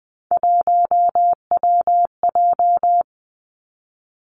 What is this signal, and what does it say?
Morse code, heard as a keyed tone, about 20 wpm, 700 Hz. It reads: 1WJ
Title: Three Challenging Letter-Number Combinations 20wpm